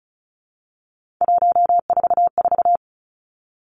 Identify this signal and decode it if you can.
Morse code, heard as a keyed tone, about 36 wpm, 700 Hz. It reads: 144
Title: Three Challenging Letter-Number Combinations 35wpm